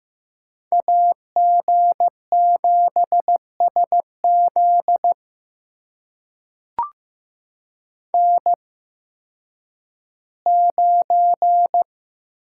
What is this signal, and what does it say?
Morse code, heard as a keyed tone, about 15 wpm, spaced as for 10 wpm, 700 Hz. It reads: AG7SZ N9
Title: USA Call Signs - 5 Characters - 8x Character Spacing - 15wpm